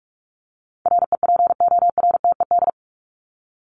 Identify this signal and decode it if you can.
Morse code, heard as a keyed tone, about 44 wpm, 700 Hz. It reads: REPORTED
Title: ED Suffix - Top 100 Words with Base Words 45wpm